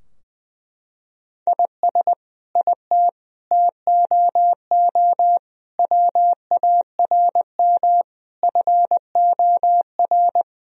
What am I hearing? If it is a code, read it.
Morse code, heard as a keyed tone, about 20 wpm, 700 Hz. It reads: IS IT TOO WARM FOR
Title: Sentences from Top 1000 Words - Review Part 2 - 20wpm